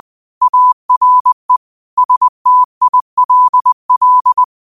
Morse code, heard as a keyed tone, about 20 wpm, 1000 Hz. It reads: ARE STILL
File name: qotdfiles / 20wpm